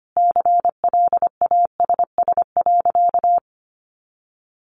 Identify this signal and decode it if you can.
Morse code, heard as a keyed tone, about 25 wpm, 700 Hz. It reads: CLASS.
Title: Additional Easy Phrases 25wpm